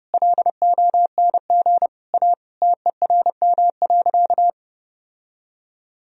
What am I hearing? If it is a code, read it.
Morse code, heard as a keyed tone, about 30 wpm, 700 Hz. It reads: LONG A TERM.
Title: Sentences from Top 1000 Words - Review Part 4 - 30wpm